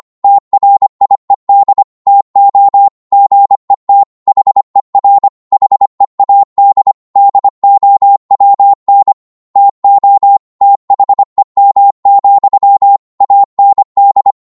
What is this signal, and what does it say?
Morse code, heard as a keyed tone, about 25 wpm, 800 Hz. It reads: TRIED TO GET HER HEAD DOWN TO THEM, AND
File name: Morse / psg-25wpm